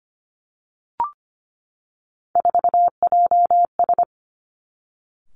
Morse code, heard as a keyed tone, about 25 wpm, 700 Hz. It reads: 4JS
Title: Three Challenging Letter-Number Combinations 25wpm